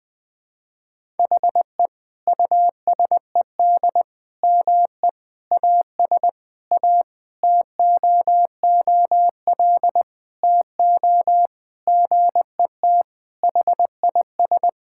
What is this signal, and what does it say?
Morse code, heard as a keyed tone, about 20 wpm, 700 Hz. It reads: HE USED ME AS A TOOL TO GET HIS
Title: Sentences from Top 1000 Words - Review Part 6 - 20wpm